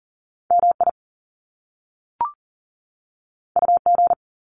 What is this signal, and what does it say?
Morse code, heard as a keyed tone, about 40 wpm, 700 Hz. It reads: MI UG
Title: Top Two Letter Combinations in Words 40wpm